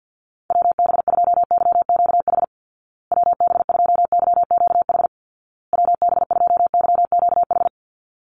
Morse code, heard as a keyed tone, about 50 wpm, 700 Hz. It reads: WB2YQH WB2YQH WB2YQH
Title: Call Signs Repeated 2x - 50wpm